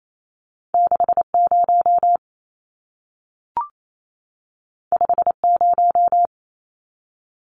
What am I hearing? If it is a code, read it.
Morse code, heard as a keyed tone, about 28 wpm, 700 Hz. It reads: 60 50
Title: Numbers - 2-Digit 28wpm